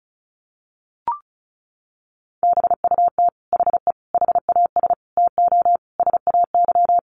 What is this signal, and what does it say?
Morse code, heard as a keyed tone, about 34 wpm, 700 Hz. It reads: BUT HE HAS TO SAY
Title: Sentences from Top 600 Words 35wpm